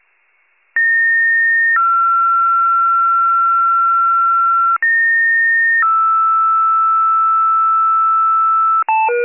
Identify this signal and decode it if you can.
Two-tone page, heard as a one-second tone, then a three-second tone, about 1850 then 1360 Hz, about 1850 then 1320 Hz. It is heard from 0.8 to 8.8 s.